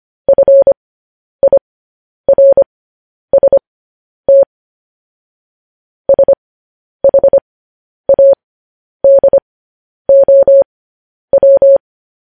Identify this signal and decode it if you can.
Morse code, heard as a keyed tone, about 25 wpm, spaced as for 10 wpm, 550 Hz. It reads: FIRST SHADOW
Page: news-headlines-in-morse-code-at-10-wpm